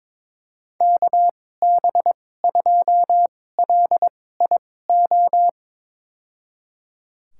Morse code, heard as a keyed tone, about 21 wpm, spaced as for 16 wpm, 700 Hz. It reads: KB2LIO
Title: USA Call Signs - 6 Characters - 2x Character Spacing - 22wpm